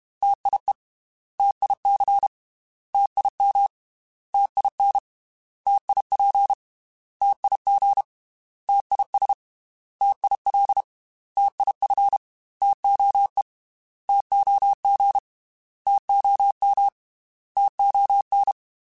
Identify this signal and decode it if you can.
Morse code, heard as a keyed tone, about 31 wpm, 800 Hz. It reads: TIE TIC TIM TIN TIP TIG TIS TIL TIF TOE TOG TOM TON
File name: T Prefixes / T mp3s